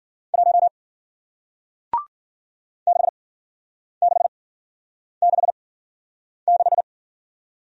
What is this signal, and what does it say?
Morse code, heard as a keyed tone, about 55 wpm, spaced as for 9 wpm, 700 Hz. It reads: 1 6666